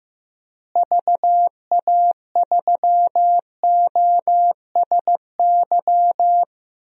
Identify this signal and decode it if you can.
Morse code, heard as a keyed tone, about 15 wpm, 700 Hz. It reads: VA3OSY
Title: DXCC-100 Call Signs - 6 Characters - Repeated 2x - 15wpm